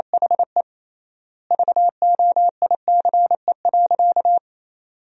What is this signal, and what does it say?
Morse code, heard as a keyed tone, about 28 wpm, 700 Hz. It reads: HE VOICE.
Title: 3x Word-Spacing - Sentences from Top 500 Words - 28wpm